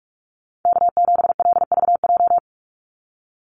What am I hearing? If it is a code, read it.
Morse code, heard as a keyed tone, about 44 wpm, 700 Hz. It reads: K7LVJ
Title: Call Signs Repeated 2x - 45wpm